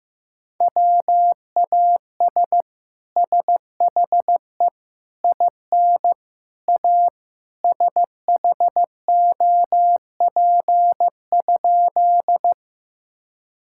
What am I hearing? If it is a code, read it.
Morse code, heard as a keyed tone, about 15 wpm, 700 Hz. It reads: WAS SHE IN A SHOP?
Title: Sentences from Top 1600 Words 15wpm